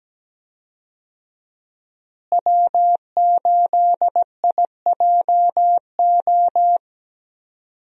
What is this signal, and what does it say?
Morse code, heard as a keyed tone, about 17 wpm, 700 Hz. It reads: W8IJO
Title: USA Call Signs - 5 Characters - Repeated 3x - 17wpm